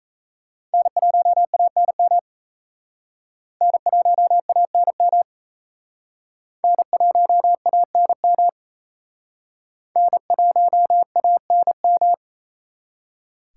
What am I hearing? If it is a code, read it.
Morse code, heard as a keyed tone, about 35 wpm, 700 Hz. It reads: N1ANM N1ANM N1ANM N1ANM
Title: Speed Racing - USA 1x3 Call Signs 28wpm